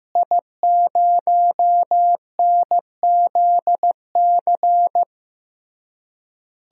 Morse code, heard as a keyed tone, about 16 wpm, 700 Hz. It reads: I0NZC
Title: DXCC-100 Call Signs - 5 Characters - Repeated 3x - 15wpm